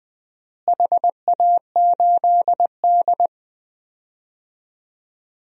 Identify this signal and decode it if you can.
Morse code, heard as a keyed tone, about 20 wpm, 700 Hz. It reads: HA8D
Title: DXCC-100 Call Signs - 4 Characters - Repeated 3x - 20wpm